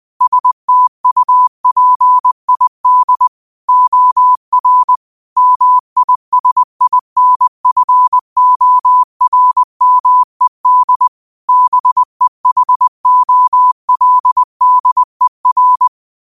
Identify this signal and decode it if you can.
Morse code, heard as a keyed tone, about 20 wpm, 1000 Hz. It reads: STUPID OR MISINFORMED BEHOLDER
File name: qotdfiles / 20wpm